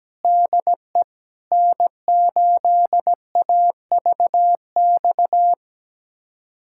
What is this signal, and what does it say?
Morse code, heard as a keyed tone, about 17 wpm, 700 Hz. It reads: DE N8AVX
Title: USA Callsign DE Callsign 17wpm